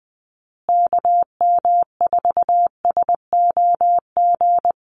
Morse code, heard as a keyed tone, about 20 wpm, 700 Hz. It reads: KM4SOG
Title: USA Call Signs Repeated 3x 20wpm